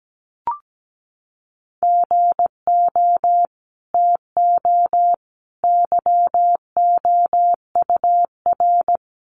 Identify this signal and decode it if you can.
Morse code, heard as a keyed tone, about 17 wpm, 700 Hz. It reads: GO TO YOUR
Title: Sentences from Top 300 Words 17wpm